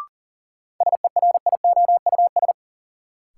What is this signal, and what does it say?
Morse code, heard as a keyed tone, about 40 wpm, 700 Hz. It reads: SERIOUS
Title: Sentences from Top 1100 Words - New Word Drill Alt - 40wpm